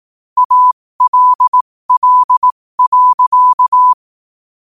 Morse code, heard as a keyed tone, about 18 wpm, spaced as for 15 wpm, 1000 Hz. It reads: ALL.
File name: qotdfiles / 15wpm